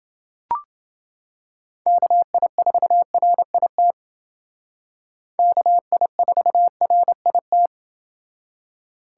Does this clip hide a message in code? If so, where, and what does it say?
Yes: Morse code from 1.9 to 7.7 s, heard as a keyed tone, about 28 wpm, 700 Hz. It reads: KI4RIT KI4RIT